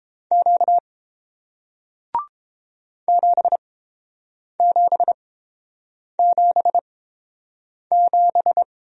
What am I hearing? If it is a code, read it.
Morse code, heard as a keyed tone, about 31 wpm, spaced as for 7 wpm, 700 Hz. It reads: Q 7777